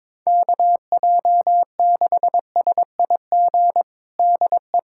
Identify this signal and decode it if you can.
Morse code, heard as a keyed tone, about 22 wpm, 700 Hz. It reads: KJ6SIG DE
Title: USA Callsign DE Callsign 22wpm